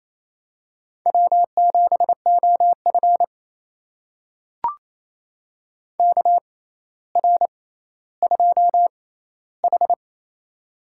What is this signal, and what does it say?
Morse code, heard as a keyed tone, about 28 wpm, spaced as for 14 wpm, 700 Hz. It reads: W7OF KR2H
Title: USA Call Signs - 4 Characters - 6x Character Spacing - 28wpm